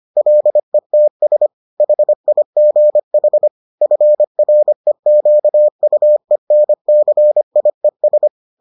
Morse code, heard as a keyed tone, about 25 wpm, 600 Hz. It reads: LETS HIGH FREQUENCIES